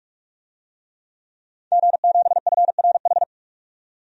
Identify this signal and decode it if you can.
Morse code, heard as a keyed tone, about 46 wpm, 700 Hz. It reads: G7FRH
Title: DXCC-100 Call Signs - 5 Characters - 45wpm